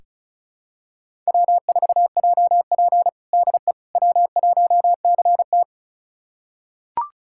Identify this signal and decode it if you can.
Morse code, heard as a keyed tone, about 34 wpm, 700 Hz. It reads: W4JP DE W1CT
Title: USA Callsign DE Callsign - 4 Characters - 35wpm